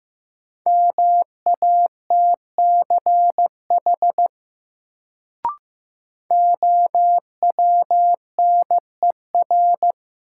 Morse code, heard as a keyed tone, about 15 wpm, 700 Hz. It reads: MATCH OWNER
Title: Sentences from Top 1600 Words - New Word Drill Alt - 15wpm